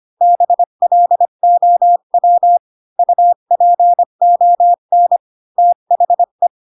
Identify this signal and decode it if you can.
Morse code, heard as a keyed tone, about 25 wpm, 700 Hz. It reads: BLOW UPON THE
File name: Pinocchio / Pinocchio ch1 20 25 ik1hge